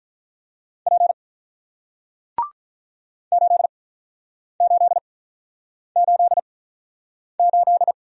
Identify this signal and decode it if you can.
Morse code, heard as a keyed tone, about 48 wpm, spaced as for 9 wpm, 700 Hz. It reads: P 8888